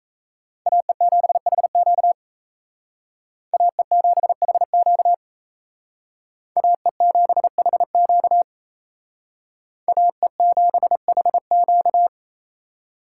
Morse code, heard as a keyed tone, about 35 wpm, 700 Hz. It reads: AE7HQ AE7HQ AE7HQ AE7HQ